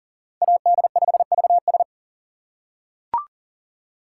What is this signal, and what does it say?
Morse code, heard as a keyed tone, about 37 wpm, 700 Hz. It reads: AD5VS
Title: USA Call Signs - 2x Character Spacing - 40wpm